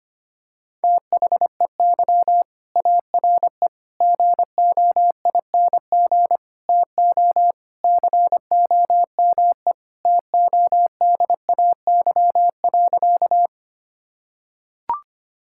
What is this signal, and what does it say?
Morse code, heard as a keyed tone, about 25 wpm, 700 Hz. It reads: THEY ARE GOING TO COME TODAY.
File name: practice / 25wpm / Sentences with 3-Word Phrases - 25wpm